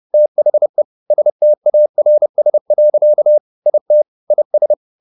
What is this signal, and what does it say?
Morse code, heard as a keyed tone, about 30 wpm, 600 Hz. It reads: THE STARS. IT IS